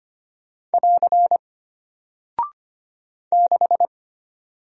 Morse code, heard as a keyed tone, about 25 wpm, 700 Hz. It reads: <AR> 6
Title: Single Character (Letter:Number:Punctuation:Prosign) Alt 25wpm